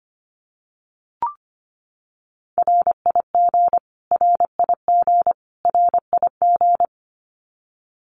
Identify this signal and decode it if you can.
Morse code, heard as a keyed tone, about 25 wpm, 700 Hz. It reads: RIG RIG RIG
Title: Abbreviations Repeated 3x - 25wpm